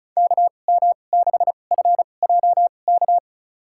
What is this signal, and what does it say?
Morse code, heard as a keyed tone, about 36 wpm, spaced as for 25 wpm, 700 Hz. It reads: KM6FJK
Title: USA Call Signs - 6 Characters - 2x Character Spacing - 35wpm